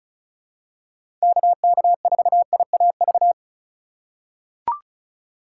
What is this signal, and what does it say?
Morse code, heard as a keyed tone, about 36 wpm, 700 Hz. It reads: KK4IAV
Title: USA Call Signs - 6 Characters - 3x Character Spacing - 35wpm